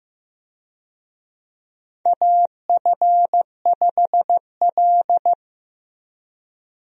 Morse code, heard as a keyed tone, about 15 wpm, 700 Hz. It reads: AF5L
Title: USA Call Signs - 4 Characters - 15wpm